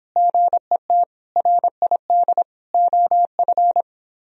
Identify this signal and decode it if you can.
Morse code, heard as a keyed tone, about 23 wpm, 700 Hz. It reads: GET RID OF